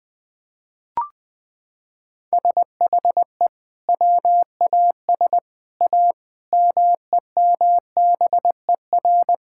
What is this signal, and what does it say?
Morse code, heard as a keyed tone, about 20 wpm, 700 Hz. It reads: SHE WAS A MEMBER
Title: Sentences with 5-Word Phrases - 20wpm